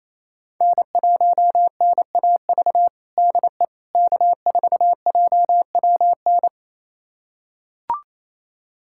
Morse code, heard as a keyed tone, about 28 wpm, 700 Hz. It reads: N1NAV DE K4JWN
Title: USA Callsign DE Callsign - 5 Characters - 28wpm